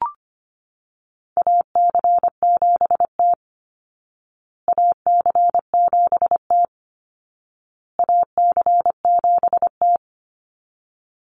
Morse code, heard as a keyed tone, about 25 wpm, 700 Hz. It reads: AC7T AC7T AC7T
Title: Call Signs Repeated 2x - 25wpm